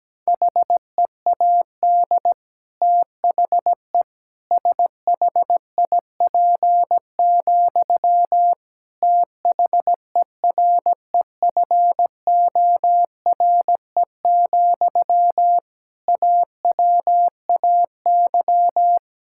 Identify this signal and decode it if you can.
Morse code, heard as a keyed tone, about 17 wpm, 700 Hz. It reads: HEAD THE SHIP, THEREFORE, AWAY
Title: Sentences from Top 1000 Words - Review Part 5 - 17wpm